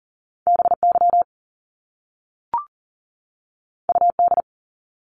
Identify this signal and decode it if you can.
Morse code, heard as a keyed tone, about 40 wpm, 700 Hz. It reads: BY UD
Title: Top Two Letter Combinations in Words 40wpm